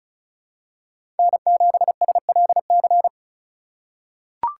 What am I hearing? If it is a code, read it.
Morse code, heard as a keyed tone, about 36 wpm, 700 Hz. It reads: N7SLC
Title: USA Call Signs - 5 Characters - 2x Character Spacing - 35wpm